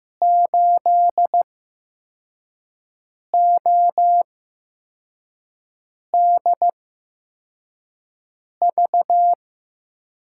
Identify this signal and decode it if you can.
Morse code, heard as a keyed tone, about 15 wpm, spaced as for 4 wpm, 700 Hz. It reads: 8ODV
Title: USA Call Signs - 5 Characters - 8x Character Spacing - 15wpm